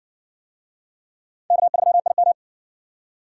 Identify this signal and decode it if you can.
Morse code, heard as a keyed tone, about 52 wpm, 700 Hz. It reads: K3IM